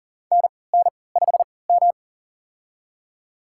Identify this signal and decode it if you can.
Morse code, heard as a keyed tone, about 40 wpm, spaced as for 23 wpm, 700 Hz. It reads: NN5M
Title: USA Call Signs - 4 Characters - 3x Character Spacing - 40wpm